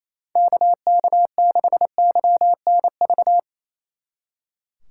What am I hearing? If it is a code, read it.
Morse code, heard as a keyed tone, about 34 wpm, 700 Hz. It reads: KK6YNV
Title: Speed Racing - USA 2x3 Call Signs 28wpm